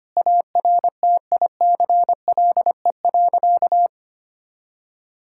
Morse code, heard as a keyed tone, about 25 wpm, 700 Hz. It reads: ARTICLE.
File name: practice / 25wpm / Sentences with 4-Word Phrases - 25wpm